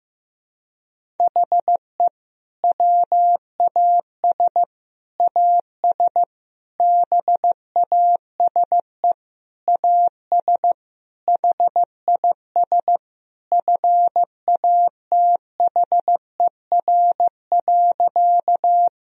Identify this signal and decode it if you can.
Morse code, heard as a keyed tone, about 15 wpm, 700 Hz. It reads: HE WAS AS BASE AS HIS FATHER.
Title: Sentences from Top 1000 Words - Review Part 1 - 15wpm